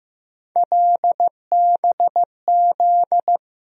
Morse code, heard as a keyed tone, about 15 wpm, 700 Hz. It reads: LBZ
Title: USA Call Sign Postfixes - 3 Letters 15wpm